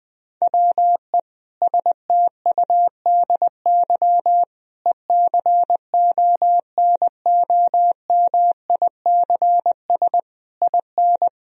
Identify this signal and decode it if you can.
Morse code, heard as a keyed tone, about 20 wpm, 700 Hz. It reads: WE STUDY ECONOMICS IN
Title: Sentences from Top 1000 Words - Review Part 3 - 20wpm